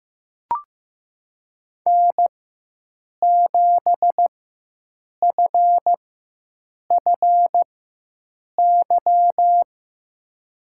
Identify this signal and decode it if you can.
Morse code, heard as a keyed tone, about 15 wpm, spaced as for 7 wpm, 700 Hz. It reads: N7FFY
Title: USA Call Signs - 5 Characters - 4x Character Spacing - 15wpm